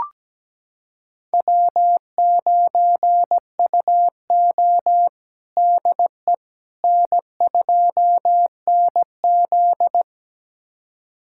Morse code, heard as a keyed tone, about 17 wpm, 700 Hz. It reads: W9UO DE N2NZ
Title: USA Callsign DE Callsign - 4 Characters - 17wpm